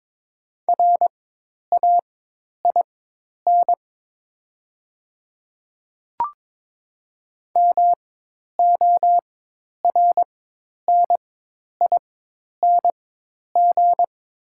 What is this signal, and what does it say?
Morse code, heard as a keyed tone, about 22 wpm, spaced as for 10 wpm, 700 Hz. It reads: RAIN MORNING
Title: 4x Character Spacing - Top 500 Words - 22wpm